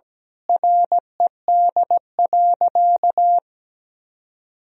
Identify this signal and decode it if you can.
Morse code, heard as a keyed tone, about 17 wpm, 700 Hz. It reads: RED.
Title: Sentences from Top 1000 Words - Review Part 2 - 17wpm